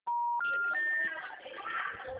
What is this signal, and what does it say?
Special information tone, with noise.